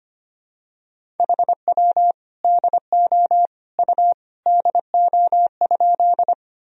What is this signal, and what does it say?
Morse code, heard as a keyed tone, about 25 wpm, 700 Hz. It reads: HW DO U DO?
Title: Sentences from Top 100 Words and Abbreviations 25wpm